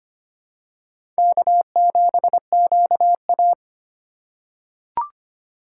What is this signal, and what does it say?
Morse code, heard as a keyed tone, about 25 wpm, 700 Hz. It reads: K7QA
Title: USA Call Signs - 4 Characters - 8x Character Spacing - 25wpm